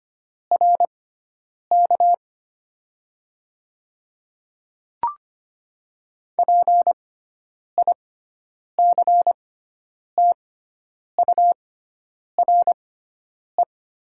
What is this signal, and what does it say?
Morse code, heard as a keyed tone, about 25 wpm, spaced as for 9 wpm, 700 Hz. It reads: RK PICTURE
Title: 6x Character Spacing - Top 300 Words - 25wpm